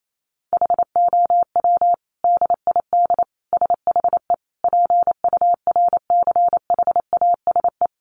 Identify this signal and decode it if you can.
Morse code, heard as a keyed tone, about 28 wpm, 700 Hz. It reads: HOW DID SHE PURCHASE
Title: Sentences from Top 800 Words 28wpm